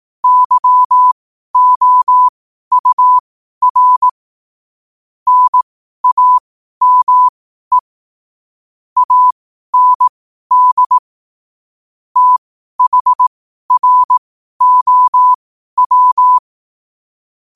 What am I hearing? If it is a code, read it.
Morse code, heard as a keyed tone, about 18 wpm, spaced as for 12 wpm, 1000 Hz. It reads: YOUR NAME AND THROW
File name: qotdfiles / 12wpm